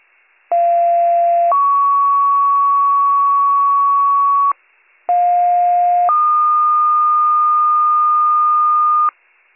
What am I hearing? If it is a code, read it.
Two-tone page, heard as a one-second tone, then a three-second tone, about 690 then 1120 Hz, about 710 then 1220 Hz. It is heard from 0.5 to 9.1 s.